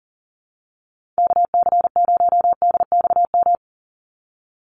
Morse code, heard as a keyed tone, about 37 wpm, 700 Hz. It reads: KC0DXM
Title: USA Call Signs 40wpm